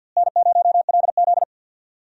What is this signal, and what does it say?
Morse code, heard as a keyed tone, about 51 wpm, 700 Hz. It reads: N0LZ
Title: USA Call Signs - 4 Characters - Repeated 2x - 50wpm